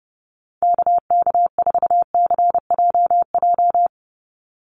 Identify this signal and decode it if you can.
Morse code, heard as a keyed tone, about 30 wpm, 700 Hz. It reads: KK4CJJ
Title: USA Call Signs 30wpm